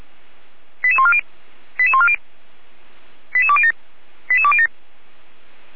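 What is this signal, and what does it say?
Five-tone selective call (ZVEI), heard as quick runs of five tones, tones of about 70 ms. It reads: 88140 88140 88207 88207
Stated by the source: ZVEI five-tone selective call 88207 received by SDR at Feuerwehr Orlamünde.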